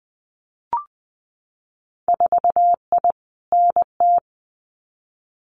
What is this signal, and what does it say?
Morse code, heard as a keyed tone, about 19 wpm, 700 Hz. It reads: 4I NT
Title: Winter Field Day Exchanges Encore 20wpm